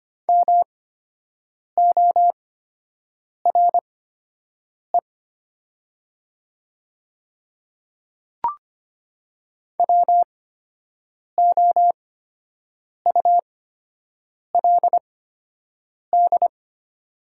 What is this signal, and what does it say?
Morse code, heard as a keyed tone, about 25 wpm, spaced as for 7 wpm, 700 Hz. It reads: MORE WOULD
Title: 8x Character Spacing - Top 100 Words - 25wpm